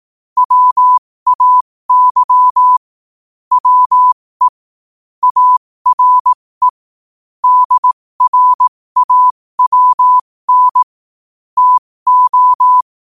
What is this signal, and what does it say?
Morse code, heard as a keyed tone, about 18 wpm, spaced as for 15 wpm, 1000 Hz. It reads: WAY WE ARE DRAWN TO
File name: qotdfiles / 15wpm